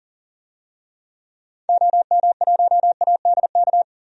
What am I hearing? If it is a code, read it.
Morse code, heard as a keyed tone, about 40 wpm, 700 Hz. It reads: OM1ADK
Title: DXCC-100 Call Signs - 6 Characters - 40wpm